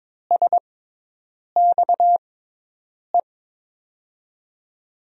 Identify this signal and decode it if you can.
Morse code, heard as a keyed tone, about 22 wpm, spaced as for 8 wpm, 700 Hz. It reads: SXE
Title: USA Call Signs - 6 Characters - 6x Character Spacing - 22wpm